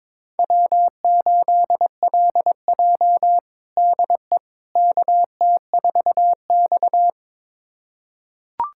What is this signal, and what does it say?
Morse code, heard as a keyed tone, about 22 wpm, 700 Hz. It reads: W8LJ DE KT4X
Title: USA Callsign DE Callsign - 4 Characters - 22wpm